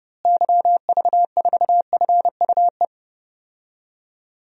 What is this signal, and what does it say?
Morse code, heard as a keyed tone, about 30 wpm, 700 Hz. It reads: YV4FUE
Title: DXCC-100 Call Signs - 6 Characters - Repeated 2x - 30wpm